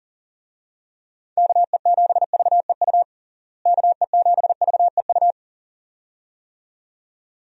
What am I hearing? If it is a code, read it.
Morse code, heard as a keyed tone, about 40 wpm, 700 Hz. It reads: KE7VEU KE7VEU
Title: USA Call Signs - 6 Characters - Repeated 2x - 40wpm